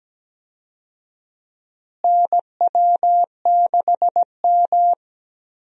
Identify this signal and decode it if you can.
Morse code, heard as a keyed tone, about 17 wpm, 700 Hz. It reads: NW6M
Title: USA Call Signs - 4 Characters - Repeated 3x - 17wpm